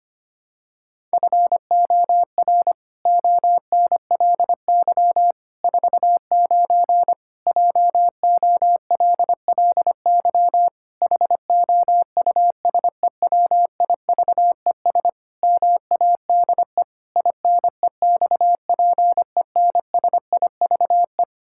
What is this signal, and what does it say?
Morse code, heard as a keyed tone, about 25 wpm, 700 Hz. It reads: FOR ONLY 49 JOLLY HOUSEWIVES MADE INEXPENSIVE
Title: Pangram Challenge v2 - 25wpm